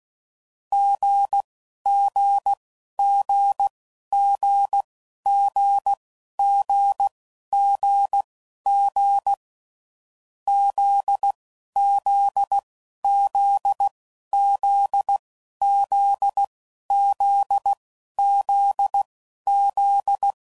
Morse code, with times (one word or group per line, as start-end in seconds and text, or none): 0.7-9.3 GGGGGGGG
10.5-20.3 ZZZZZZZZ